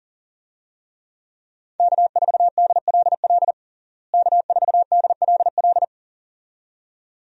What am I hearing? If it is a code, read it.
Morse code, heard as a keyed tone, about 37 wpm, 700 Hz. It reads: K4DLL K4DLL
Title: USA Call Signs - 5 Characters - Repeated 2x - 40wpm